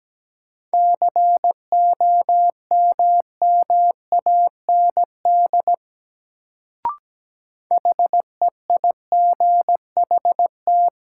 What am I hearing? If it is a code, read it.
Morse code, heard as a keyed tone, about 17 wpm, 700 Hz. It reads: COMMAND HEIGHT
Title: Sentences from Top 1600 Words - New Word Drill - 17wpm